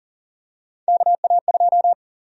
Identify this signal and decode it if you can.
Morse code, heard as a keyed tone, about 39 wpm, 700 Hz. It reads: KA2
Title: USA Call Sign Prefixes - 40wpm